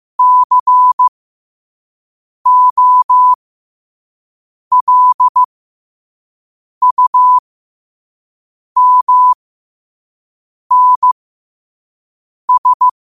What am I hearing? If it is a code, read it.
Morse code, heard as a keyed tone, about 15 wpm, spaced as for 5 wpm, 1000 Hz. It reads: COLUMNS
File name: qotdfiles / 5wpm